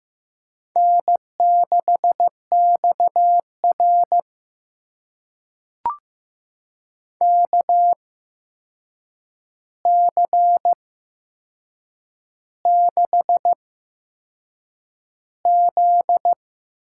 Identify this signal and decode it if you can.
Morse code, heard as a keyed tone, about 16 wpm, spaced as for 7 wpm, 700 Hz. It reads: N6XR KC6Z
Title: USA Call Signs - 8x Character Spacing - 15wpm